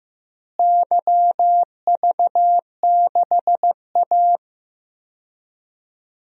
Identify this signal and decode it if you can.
Morse code, heard as a keyed tone, about 15 wpm, 700 Hz. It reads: YV6A
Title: DXCC-100 Call Signs - 4 Characters - Repeated 2x - 15wpm